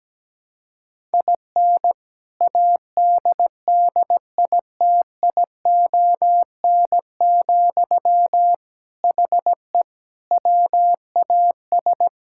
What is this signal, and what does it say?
Morse code, heard as a keyed tone, about 17 wpm, 700 Hz. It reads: IN ADDITION, HE WAS
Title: Sentences from Top 1000 Words - Review Part 6 - 17wpm